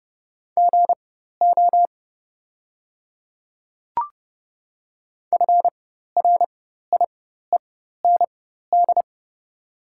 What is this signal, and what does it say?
Morse code, heard as a keyed tone, about 30 wpm, spaced as for 14 wpm, 690 Hz. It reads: GO FRIEND
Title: 4x Character Spacing - Top 300 Words - 30wpm